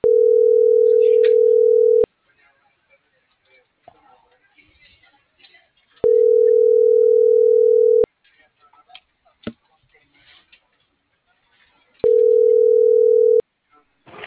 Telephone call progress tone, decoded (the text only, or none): ringback tone